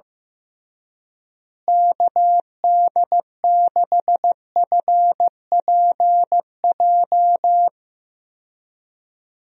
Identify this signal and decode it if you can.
Morse code, heard as a keyed tone, about 15 wpm, 700 Hz. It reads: KD6FPJ
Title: USA Call Signs - 6 Characters - Repeated 3x - 15wpm